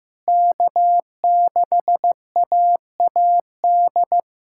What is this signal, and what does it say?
Morse code, heard as a keyed tone, about 15 wpm, 700 Hz. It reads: K6AAD
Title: SOTA QSO Practice 15wpm